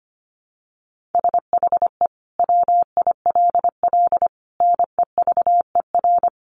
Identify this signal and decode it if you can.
Morse code, heard as a keyed tone, about 25 wpm, 700 Hz. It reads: SHE WILL NEVER
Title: Sentences from Top 1300 Words 25wpm